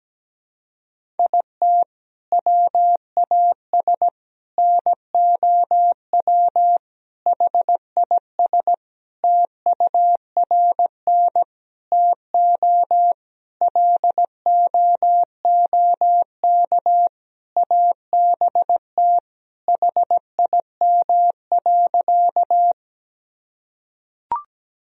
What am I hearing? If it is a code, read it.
Morse code, heard as a keyed tone, about 17 wpm, 700 Hz. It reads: IT WAS NOW HIS TURN TO LOOK ABT HIM.
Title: Sentences from Top 200 Words and Abbreviations 17wpm